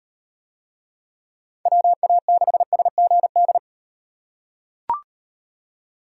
Morse code, heard as a keyed tone, about 36 wpm, 700 Hz. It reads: WA6SGD